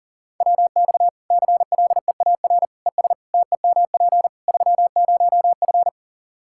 Morse code, heard as a keyed tone, about 38 wpm, 700 Hz. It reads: WX CLEAR ES TEMP 30F
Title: Practice QSO - Exchange 2 - Info - 40wpm